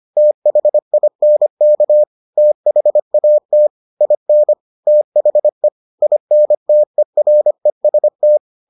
Morse code, heard as a keyed tone, about 25 wpm, 600 Hz. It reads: THINK THAT IN THE INTEREST